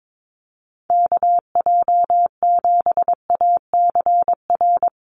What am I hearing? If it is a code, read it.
Morse code, heard as a keyed tone, about 22 wpm, 700 Hz. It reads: KJ7ACR
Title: USA Call Signs Repeated 2x 22wpm